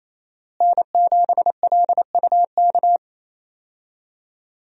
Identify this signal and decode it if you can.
Morse code, heard as a keyed tone, about 27 wpm, 700 Hz. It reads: N7LUK
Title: USA Call Signs - 5 Characters - Repeated 3x - 28wpm